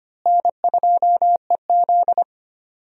Morse code, heard as a keyed tone, about 25 wpm, 700 Hz. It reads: N2EZ
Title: USA Call Signs - 4 Characters - Repeated 3x - 25wpm